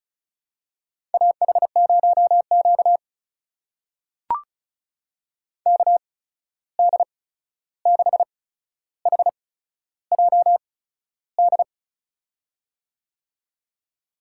Morse code, heard as a keyed tone, about 36 wpm, spaced as for 13 wpm, 700 Hz. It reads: AH0Q KD6HJD
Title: USA Call Signs - 8x Character Spacing - 35wpm